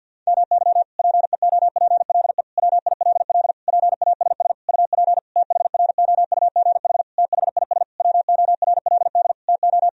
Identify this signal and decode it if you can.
Morse code, heard as a keyed tone, about 52 wpm, 700 Hz. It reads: MY PEOPLE WILL PASS UP THROUGH THIS WORLD TO
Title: Sentences from Top 1000 Words - Review Part 2 - 50wpm